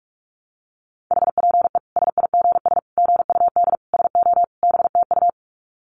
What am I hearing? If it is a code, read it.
Morse code, heard as a keyed tone, about 44 wpm, 700 Hz. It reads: HPE SIGS GUD SO BTU
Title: Common QSO Phrases 45wpm